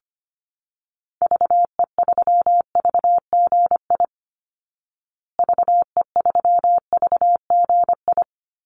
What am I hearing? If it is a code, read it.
Morse code, heard as a keyed tone, about 25 wpm, 700 Hz. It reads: VE3VGI VE3VGI
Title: Call Signs Repeated 3x - 25wpm